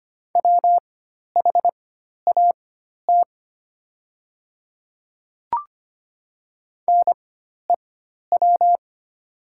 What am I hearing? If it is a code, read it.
Morse code, heard as a keyed tone, about 25 wpm, spaced as for 12 wpm, 700 Hz. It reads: WHAT NEW
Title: 4x Character Spacing - Top 200 Words - 25wpm